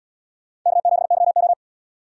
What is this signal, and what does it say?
Morse code, heard as a keyed tone, about 76 wpm, 700 Hz. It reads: K7YO